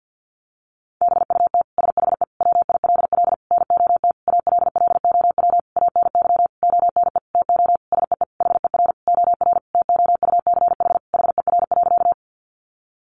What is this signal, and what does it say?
Morse code, heard as a keyed tone, about 50 wpm, 700 Hz. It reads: BUT SHE WILL NOT ALLOW ANY ONE TO SEE HER OR TOUCH HER.
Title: Sentences from Top 1200 Words 50wpm